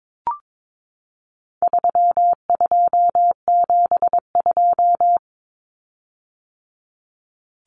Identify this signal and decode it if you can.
Morse code, heard as a keyed tone, about 22 wpm, 700 Hz. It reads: 3272
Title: Common Cut Numbers - 4-Digit 22wpm